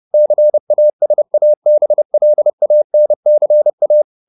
Morse code, heard as a keyed tone, about 30 wpm, 600 Hz. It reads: CASABLANCA
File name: the old man and the sea / the old man and the sea 1 30wpm